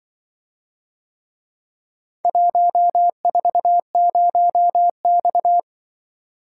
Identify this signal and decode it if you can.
Morse code, heard as a keyed tone, about 23 wpm, 700 Hz. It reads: 140X
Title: Speed-Racing - Four Letter-Number Combinations - 20wpm